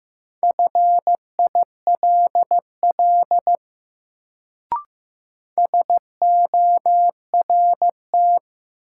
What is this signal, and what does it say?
Morse code, heard as a keyed tone, about 15 wpm, 700 Hz. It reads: FILL SORT
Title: Sentences from Top 1000 Words - Review Part 3 - New Word Drill - 15wpm